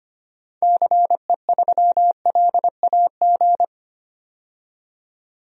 Morse code, heard as a keyed tone, about 25 wpm, 700 Hz. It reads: CE3LAG
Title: DXCC-100 Call Signs - 6 Characters - 25wpm